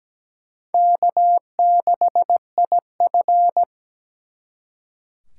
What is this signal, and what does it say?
Morse code, heard as a keyed tone, about 20 wpm, 700 Hz. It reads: K6IF
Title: Speed Racing - USA Call Signs 17wpm